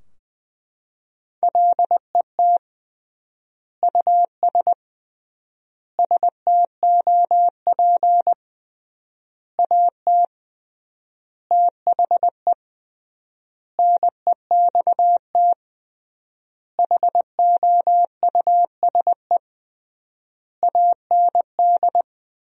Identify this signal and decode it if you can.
Morse code, heard as a keyed tone, about 20 wpm, 700 Hz. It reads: LET US STOP AT THE NEXT HOUSE AND
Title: 3x Word-Spacing - Sentences from Top 400 Words - 20wpm